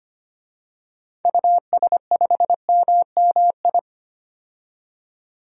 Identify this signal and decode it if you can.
Morse code, heard as a keyed tone, about 25 wpm, 700 Hz. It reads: US5MMI
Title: DXCC-100 Call Signs - 6 Characters - 25wpm